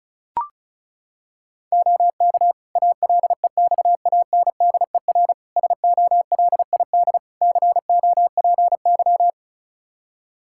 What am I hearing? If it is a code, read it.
Morse code, heard as a keyed tone, about 36 wpm, 700 Hz. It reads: OK ALEXANDER SOLID COPY
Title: K1USN SST QSO Practice 35wpm